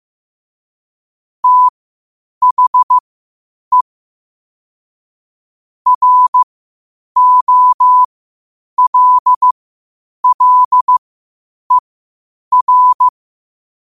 Morse code, heard as a keyed tone, about 15 wpm, spaced as for 8 wpm, 1000 Hz. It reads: THE ROLLER
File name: qotdfiles / 8wpm